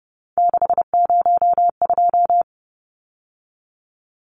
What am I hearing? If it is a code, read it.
Morse code, heard as a keyed tone, about 30 wpm, 700 Hz. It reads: 602
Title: Numbers - 3-Digit 30wpm